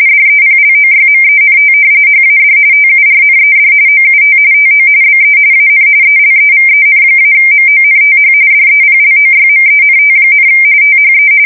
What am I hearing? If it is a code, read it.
RTTY, about 45.45 baud. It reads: DEPLOYMENT HAD TURNED OUT TO BE AN ELEVEN-MONTH DEPLOYMENT, THE
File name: fsk170 / battleshipnj